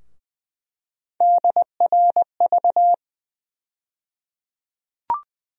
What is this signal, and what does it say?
Morse code, heard as a keyed tone, about 20 wpm, 700 Hz. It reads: DRV
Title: USA Call Sign Postfixes - 3 Letters 20wpm